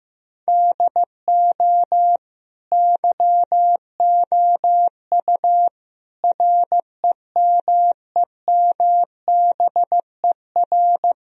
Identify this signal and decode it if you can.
Morse code, heard as a keyed tone, about 16 wpm, 700 Hz. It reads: DO YOU REMEMBER
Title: Sentences from Top 1000 Words - Review Part 1 - 15wpm